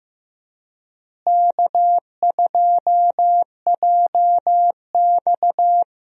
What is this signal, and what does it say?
Morse code, heard as a keyed tone, about 15 wpm, 700 Hz. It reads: K2JX
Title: USA Call Signs - 4 Characters - Repeated 2x - 15wpm